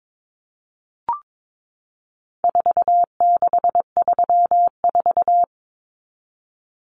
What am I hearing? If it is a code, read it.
Morse code, heard as a keyed tone, about 22 wpm, 700 Hz. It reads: 4634
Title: Numbers - 4-Digit 22wpm